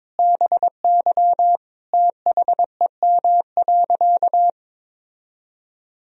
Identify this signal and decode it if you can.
Morse code, heard as a keyed tone, about 22 wpm, 700 Hz. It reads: BY THEM.
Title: Sentences from Top 1000 Words - Review Part 1 - 22wpm